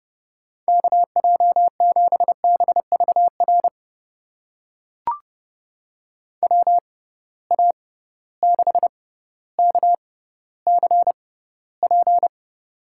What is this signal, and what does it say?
Morse code, heard as a keyed tone, about 30 wpm, spaced as for 15 wpm, 700 Hz. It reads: KJ7BVR WA6KCP
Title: USA Call Signs - 6 Characters - 6x Character Spacing - 30wpm